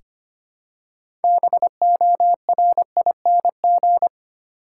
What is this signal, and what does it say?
Morse code, heard as a keyed tone, about 25 wpm, 700 Hz. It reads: BORING
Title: Six Letter Words - 25wpm